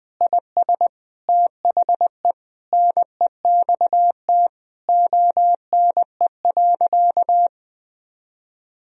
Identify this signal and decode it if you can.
Morse code, heard as a keyed tone, about 20 wpm, 700 Hz. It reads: IS THE NEXT ONE.
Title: Sentences from Top 1000 Words - Review Part 1 - 20wpm